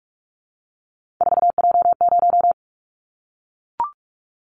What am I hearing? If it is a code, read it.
Morse code, heard as a keyed tone, about 44 wpm, 700 Hz. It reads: 4J0
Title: Callsign Prefixes for Top 100 DXCC - 45wpm